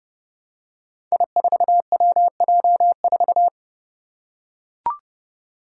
Morse code, heard as a keyed tone, about 28 wpm, 700 Hz. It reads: I4WJ4
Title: Five Challenging Letter-Number Combinations 30wpm